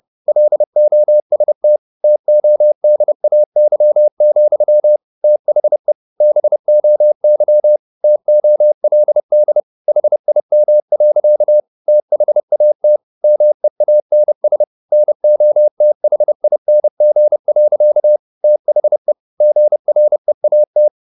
Morse code, heard as a keyed tone, about 30 wpm, 600 Hz. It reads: LOST TODAY, THE BOY TOLD HIM. THAT MEANS NOTHING. THE GREAT
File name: the old man and the sea / the old man and the sea 1 30wpm